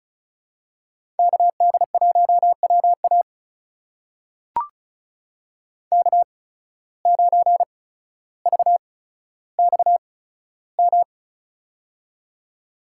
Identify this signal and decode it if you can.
Morse code, heard as a keyed tone, about 34 wpm, spaced as for 15 wpm, 700 Hz. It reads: KD1WA K9VXM
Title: USA Call Signs - 5 Characters - 8x Character Spacing - 35wpm